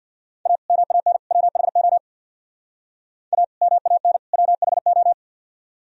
Morse code, heard as a keyed tone, about 52 wpm, 700 Hz. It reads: A MAN WHO A MAN WHO
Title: Speed-Racing - Sets of 3 Words 50wpm